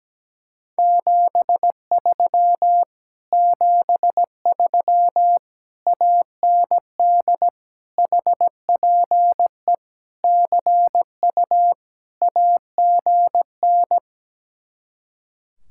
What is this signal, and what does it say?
Morse code, heard as a keyed tone, about 17 wpm, 700 Hz. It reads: 73 73 AND HPE CU AGN
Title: Speed-Racing Common QSO Phrases Encore - 17wpm